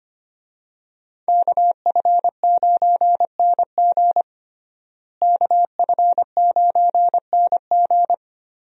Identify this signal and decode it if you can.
Morse code, heard as a keyed tone, about 25 wpm, 700 Hz. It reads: KF9NG KF9NG
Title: USA Call Signs - 5 Characters - Repeated 3x - 25wpm